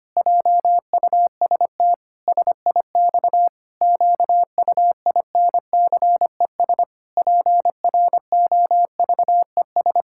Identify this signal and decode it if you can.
Morse code, heard as a keyed tone, about 25 wpm, 700 Hz. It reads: JUST SIX QUINCES PROVES
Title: Pangram Challenge v2 - 25wpm